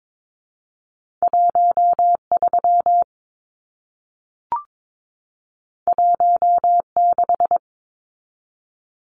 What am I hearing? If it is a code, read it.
Morse code, heard as a keyed tone, about 22 wpm, 700 Hz. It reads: 13 16
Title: Numbers - 2-Digit 22wpm